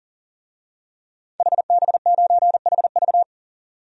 Morse code, heard as a keyed tone, about 37 wpm, 700 Hz. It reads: HB9HV
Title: DXCC-100 Call Signs - 5 Characters - 40wpm